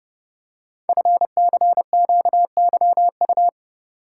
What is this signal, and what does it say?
Morse code, heard as a keyed tone, about 30 wpm, 690 Hz. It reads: FCQYU
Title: Code Groups - 5 Letters - 30wpm